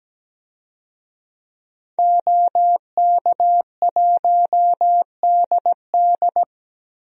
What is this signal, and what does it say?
Morse code, heard as a keyed tone, about 17 wpm, 700 Hz. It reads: OK1DD
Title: DXCC-100 Call Signs - 5 Characters - Repeated 2x - 17wpm